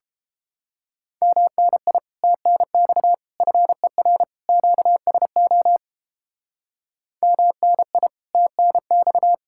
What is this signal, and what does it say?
Morse code, heard as a keyed tone, about 33 wpm, 700 Hz. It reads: MNI TNX FER QSO MNI TNX